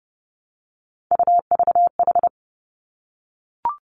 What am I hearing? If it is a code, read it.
Morse code, heard as a keyed tone, about 30 wpm, 700 Hz. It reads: UVH
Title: Three Challenging Letter-Number Combinations 30wpm